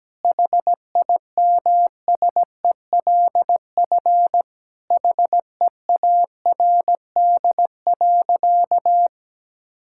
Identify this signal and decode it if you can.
Morse code, heard as a keyed tone, about 17 wpm, 700 Hz. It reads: HIMSELF HEARD.
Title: Sentences from Top 1600 Words 17wpm